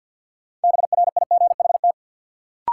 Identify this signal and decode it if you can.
Morse code, heard as a keyed tone, about 51 wpm, 700 Hz. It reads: BRIGHT
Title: Sentences from Top 1000 Words - New Word Drill - 50wpm